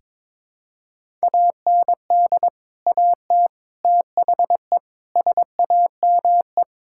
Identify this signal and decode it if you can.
Morse code, heard as a keyed tone, about 22 wpm, 700 Hz. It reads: AND AT THE SAME
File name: practice / 22wpm / Build A Phrase - 5-Word Phrases - 22wpm